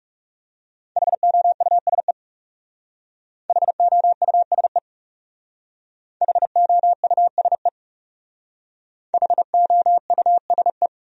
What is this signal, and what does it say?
Morse code, heard as a keyed tone, about 39 wpm, 700 Hz. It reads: HOUSE HOUSE HOUSE HOUSE